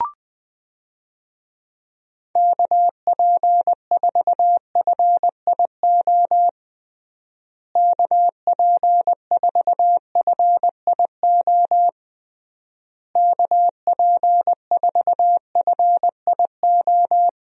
Morse code, heard as a keyed tone, about 20 wpm, 700 Hz. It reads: KP4FIO KP4FIO KP4FIO
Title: USA Call Signs - 6 Characters - Repeated 3x - 20wpm